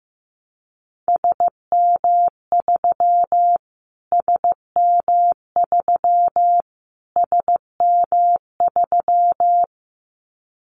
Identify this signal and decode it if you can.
Morse code, heard as a keyed tone, about 15 wpm, 700 Hz. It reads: SM3 SM3 SM3
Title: Callsign Prefixes for Top 25 DXCC - Repeated 3x - 15wpm